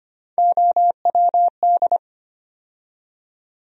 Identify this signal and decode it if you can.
Morse code, heard as a keyed tone, about 25 wpm, 700 Hz. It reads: OWD
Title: USA Call Sign Postfixes - 3 Letters 25wpm